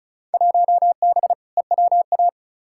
Morse code, heard as a keyed tone, about 36 wpm, 700 Hz. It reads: 1B EWA
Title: Field Day Exchanges Encore - 35wpm